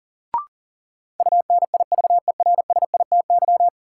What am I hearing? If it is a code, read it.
Morse code, heard as a keyed tone, about 40 wpm, 700 Hz. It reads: UNIVERSITY
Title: Sentences from Top 1000 Words - Review Part 4 - New Word Drill - 40wpm